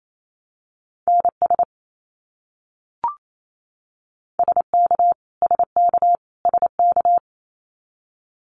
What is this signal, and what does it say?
Morse code, heard as a keyed tone, about 28 wpm, 700 Hz. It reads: NS SK SK SK
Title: Canadian Provinces and Territories - Repeated 3x - 28wpm